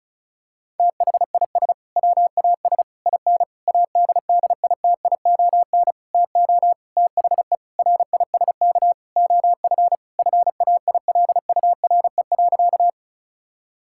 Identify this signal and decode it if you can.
Morse code, heard as a keyed tone, about 36 wpm, 700 Hz. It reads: THIS WAS IN ADDITION TO THE RISK OF FAILURE.
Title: Sentences from Top 1000 Words - Review Part 6 - 35wpm